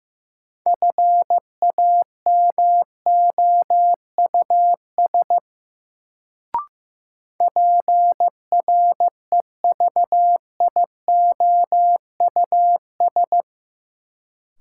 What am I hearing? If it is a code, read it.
Morse code, heard as a keyed tone, about 16 wpm, 700 Hz. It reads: FAMOUS PREVIOUS
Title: Sentences from Top 1500 Words - New Word Drill - 15wpm